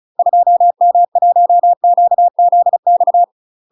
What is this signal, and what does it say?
Morse code, heard as a keyed tone, about 34 wpm, 700 Hz. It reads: 2M1QZX
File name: UK MASTER QSOs TRAINING / 35WPM / UK MASTER QSOs 0011